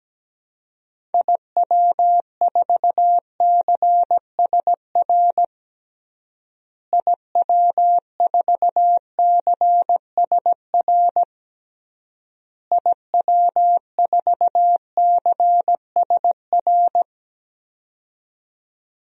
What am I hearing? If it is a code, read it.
Morse code, heard as a keyed tone, about 17 wpm, 700 Hz. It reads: IW4CSR IW4CSR IW4CSR
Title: DXCC-100 Call Signs - 6 Characters - Repeated 3x - 17wpm